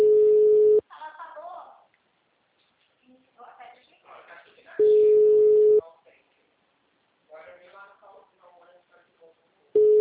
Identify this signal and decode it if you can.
Ringback tone (Europe, 425 Hz), with speech.
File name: out- -2011- - - .3026